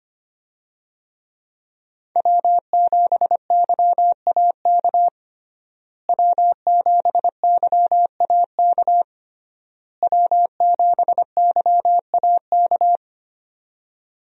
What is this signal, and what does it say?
Morse code, heard as a keyed tone, about 25 wpm, 700 Hz. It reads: W7YAK W7YAK W7YAK
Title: USA Call Signs - 5 Characters - Repeated 3x - 25wpm